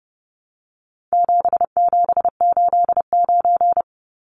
Morse code, heard as a keyed tone, about 30 wpm, 700 Hz. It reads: 7789
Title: Numbers - 4-Digit 30wpm